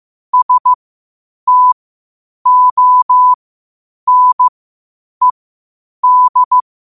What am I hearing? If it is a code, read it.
Morse code, heard as a keyed tone, about 15 wpm, spaced as for 8 wpm, 1000 Hz. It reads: STONED
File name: qotdfiles / 8wpm